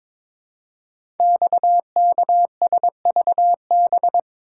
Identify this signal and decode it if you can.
Morse code, heard as a keyed tone, about 22 wpm, 700 Hz. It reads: XKSVB
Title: Code Groups - 5 Letters - 22wpm